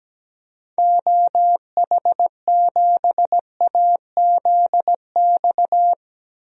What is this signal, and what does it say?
Morse code, heard as a keyed tone, about 17 wpm, 700 Hz. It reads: OH7AZX
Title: DXCC-100 Call Signs - 6 Characters - Repeated 2x - 17wpm